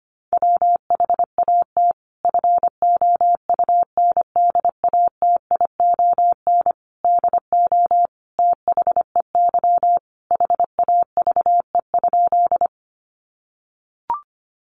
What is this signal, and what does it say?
Morse code, heard as a keyed tone, about 25 wpm, 700 Hz. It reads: WHAT FOUNDATION DO THEY HAVE?
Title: Sentences from Top 1200 Words 25wpm